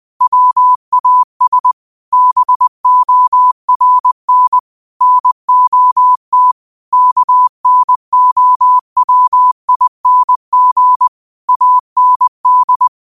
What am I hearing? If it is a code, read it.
Morse code, heard as a keyed tone, about 20 wpm, 1000 Hz. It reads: WAS BORN NOT KNOWING AND
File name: qotdfiles / 20wpm